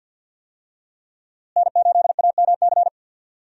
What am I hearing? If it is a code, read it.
Morse code, heard as a keyed tone, about 46 wpm, 700 Hz. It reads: N8AMC
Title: USA Call Signs - 5 Characters - Repeated 3x - 50wpm